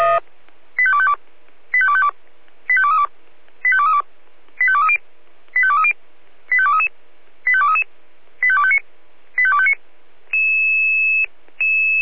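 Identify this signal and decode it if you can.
Five-tone selective call (ZVEI), heard as quick runs of five tones, tones of about 70 ms. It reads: 85252 85252 85232 85232 85230 85230 85233 85233 85259 85259
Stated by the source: ZVEI five-tone call 85885 received by SDR.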